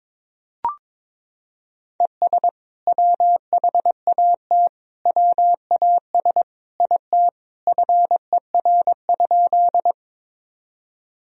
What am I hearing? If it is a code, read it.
Morse code, heard as a keyed tone, about 22 wpm, 700 Hz. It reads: ES WHAT WAS IT FER?
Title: Sentences from Top 100 Words and Abbreviations 22wpm